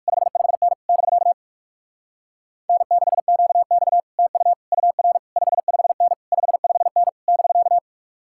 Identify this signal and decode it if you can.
Morse code, heard as a keyed tone, about 60 wpm, spaced as for 53 wpm, 700 Hz. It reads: 55N <BK> N6QX TU UR 55N 55N <BK>
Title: Speed-Racing Common QSO Phrases - 45wpm